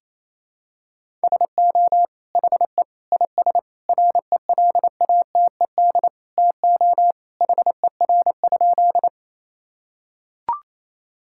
Morse code, heard as a keyed tone, about 28 wpm, 700 Hz. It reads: SO HE IS RELATED TO HER?
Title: Sentences from Top 1000 Words - Review Part 5 - 28wpm